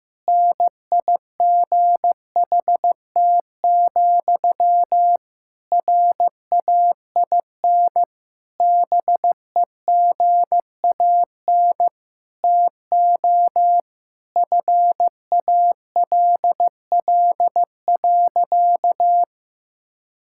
Morse code, heard as a keyed tone, about 15 wpm, 700 Hz. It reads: NIGHT, RAIN BEGAN TO FALL.
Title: Sentences from Top 1000 Words - Review Part 6 - 15wpm